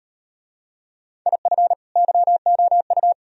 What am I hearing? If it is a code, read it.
Morse code, heard as a keyed tone, about 36 wpm, 700 Hz. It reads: IF YOU